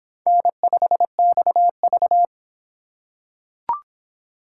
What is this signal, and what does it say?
Morse code, heard as a keyed tone, about 22 wpm, 700 Hz. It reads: N5XV